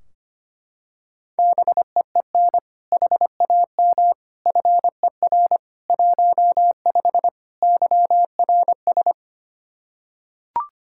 Morse code, heard as a keyed tone, about 25 wpm, 700 Hz. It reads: BEEN HAM FER 15 YRS
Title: K1USN SST QSO Practice 25wpm